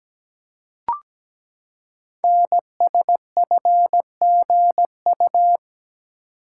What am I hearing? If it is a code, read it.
Morse code, heard as a keyed tone, about 17 wpm, 700 Hz. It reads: NSFGU
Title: Code Groups - 5 Letters - 17wpm